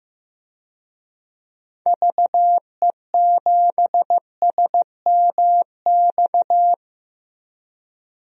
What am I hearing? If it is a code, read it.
Morse code, heard as a keyed tone, about 15 wpm, 700 Hz. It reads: VE7SMX
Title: DXCC-100 Call Signs - 6 Characters - Repeated 2x - 15wpm